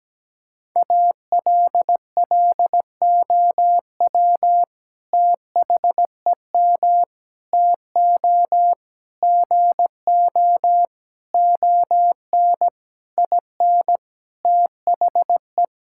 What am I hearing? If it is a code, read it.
Morse code, heard as a keyed tone, about 17 wpm, 700 Hz. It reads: ALLOW THEM TO GO ON IN THE
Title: Sentences from Top 1000 Words - Review Part 5 - 17wpm